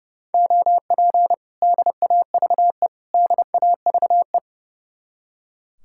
Morse code, heard as a keyed tone, about 36 wpm, 700 Hz. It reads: OP DAVE DAVE
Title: Speed-Racing Common QSO Phrases - 30wpm